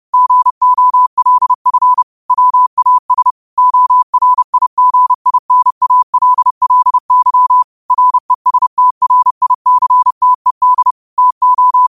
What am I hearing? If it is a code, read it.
Morse code, heard as a keyed tone, about 30 wpm, 1000 Hz. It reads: GOLF WAS ORIGINALLY RESTRICTED TO
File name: qotdfiles / 30wpm